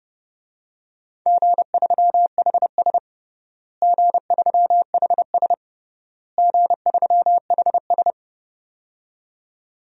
Morse code, heard as a keyed tone, about 30 wpm, 700 Hz. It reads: G3HS G3HS G3HS
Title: DXCC-100 Call Signs - 4 Characters - Repeated 3x - 30wpm